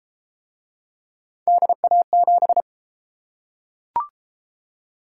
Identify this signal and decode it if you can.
Morse code, heard as a keyed tone, about 30 wpm, 700 Hz. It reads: DA7